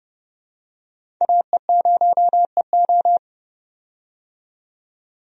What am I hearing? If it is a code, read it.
Morse code, heard as a keyed tone, about 30 wpm, 700 Hz. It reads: AE0EO
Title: USA Call Signs - 5 Characters - 30wpm